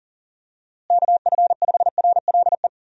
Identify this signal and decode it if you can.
Morse code, heard as a keyed tone, about 40 wpm, 700 Hz. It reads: KF5RLE
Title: USA Call Signs - 6 Characters - Repeated 3x - 40wpm